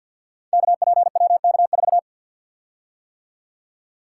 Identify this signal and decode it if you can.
Morse code, heard as a keyed tone, about 47 wpm, 700 Hz. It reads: KPWK4
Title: Code Groups - 5 Letter:Numbers - 50wpm